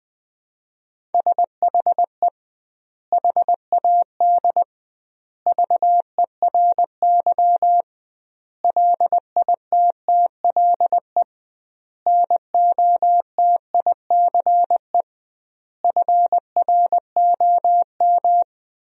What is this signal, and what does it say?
Morse code, heard as a keyed tone, about 21 wpm, 700 Hz. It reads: SHE HAD VERY LITTLE NOTICE FROM
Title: 2x Word-Spacing - Sentences from Top 400 Words - 20wpm